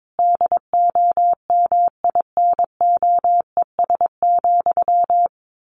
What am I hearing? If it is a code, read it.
Morse code, heard as a keyed tone, about 22 wpm, 700 Hz. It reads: DOMINOES,
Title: A Sound Of Thunder 22wpm